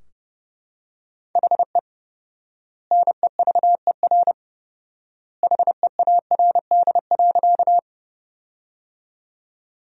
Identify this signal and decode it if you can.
Morse code, heard as a keyed tone, about 30 wpm, 700 Hz. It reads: HE NEVER HEARD.
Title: 4x Word-Spacing - Sentences from Top 400 Words - 30wpm